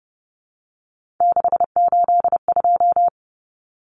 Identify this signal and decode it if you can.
Morse code, heard as a keyed tone, about 29 wpm, 700 Hz. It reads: 682
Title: Common Cut Numbers - 3-Digit 30wpm